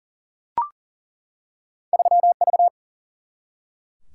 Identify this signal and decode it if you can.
Morse code, heard as a keyed tone, about 40 wpm, 700 Hz. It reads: 3 V
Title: Single-Letter Single-Number 40wpm